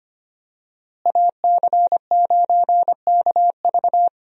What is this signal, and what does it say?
Morse code, heard as a keyed tone, about 25 wpm, 700 Hz. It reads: AC9KV
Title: USA Call Signs - 5 Characters - Repeated 2x - 25wpm